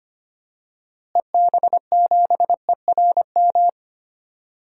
Morse code, heard as a keyed tone, about 25 wpm, 700 Hz. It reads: EB7ERM
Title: DXCC-100 Call Signs - 6 Characters - Repeated 2x - 25wpm